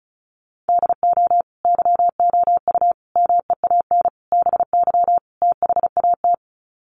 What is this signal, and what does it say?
Morse code, heard as a keyed tone, about 34 wpm, 700 Hz. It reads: DO YOU MEAN BY THAT
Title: Sets of 5 Words Encore2 35wpm